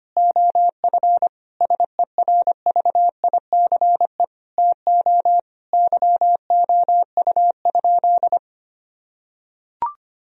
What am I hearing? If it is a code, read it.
Morse code, heard as a keyed tone, about 25 wpm, 700 Hz. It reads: OF SERVICE TO YOU?
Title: Sentences from Top 1000 Words - Review Part 4 - 25wpm